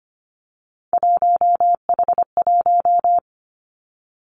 Morse code, heard as a keyed tone, about 25 wpm, 700 Hz. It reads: 1H1
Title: Three Challenging Letter-Number Combinations 25wpm